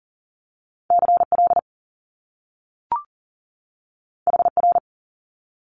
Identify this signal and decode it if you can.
Morse code, heard as a keyed tone, about 40 wpm, 700 Hz. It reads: CL HR
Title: Top Two Letter Combinations in Words 40wpm